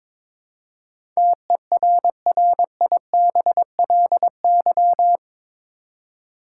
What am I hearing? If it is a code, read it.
Morse code, heard as a keyed tone, about 22 wpm, 700 Hz. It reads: TERRIBLY
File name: practice / 22wpm / Eight Letter Words - 22wpm